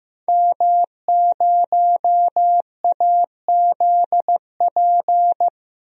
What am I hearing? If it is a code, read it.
Morse code, heard as a keyed tone, about 15 wpm, 700 Hz. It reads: M0AZP
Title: DXCC-100 Call Signs - 5 Characters - 15wpm